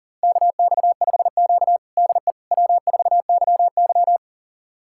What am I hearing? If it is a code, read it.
Morse code, heard as a keyed tone, about 40 wpm, 700 Hz. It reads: KX5Q DE W4YY
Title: USA Callsign DE Callsign - 4 Characters - 40wpm